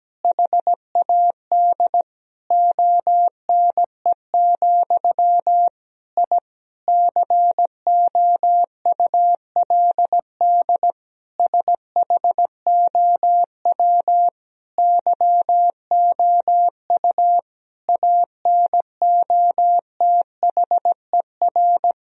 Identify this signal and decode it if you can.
Morse code, heard as a keyed tone, about 17 wpm, 700 Hz. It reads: HAD ONE, I COULD SHOW YOU ANOTHER
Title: Sentences from Top 1000 Words - Review Part 3 - 17wpm